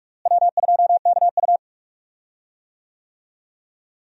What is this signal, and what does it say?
Morse code, heard as a keyed tone, about 46 wpm, 700 Hz. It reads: W2KU
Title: USA Call Signs - 4 Characters - Repeated 3x - 45wpm